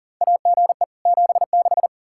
Morse code, heard as a keyed tone, about 40 wpm, 700 Hz. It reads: AGE 76
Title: Practice QSO - Exchange 3 - More Info - 40wpm